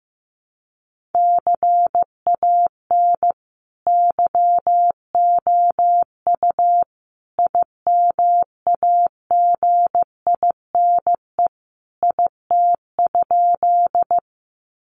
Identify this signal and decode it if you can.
Morse code, heard as a keyed tone, about 15 wpm, 700 Hz. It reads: CAN YOU IMAGINE IT?
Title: Sentences from Top 1400 Words 15wpm